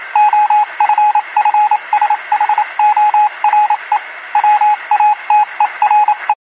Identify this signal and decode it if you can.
Morse code, heard as a keyed tone, about 27 wpm, 840 Hz, with radio noise. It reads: OFFSHORE WATERE
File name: pre-comm morse-code-receive 6032-60-1